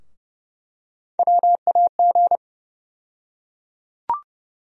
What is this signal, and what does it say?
Morse code, heard as a keyed tone, about 30 wpm, 700 Hz. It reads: WAG
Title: USA Call Sign Postfixes - 3 Letters 30wpm